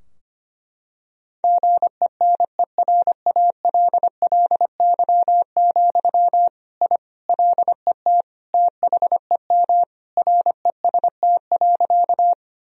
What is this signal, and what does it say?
Morse code, heard as a keyed tone, about 25 wpm, 700 Hz. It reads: GENERALLY, I LET THEM REST.
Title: Sentences from Top 1000 Words - Review Part 4 - 25wpm